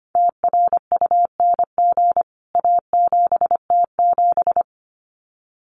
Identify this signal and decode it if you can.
Morse code, heard as a keyed tone, about 25 wpm, 700 Hz. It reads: TRUNG A7T7
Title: CWT Exchange 25wpm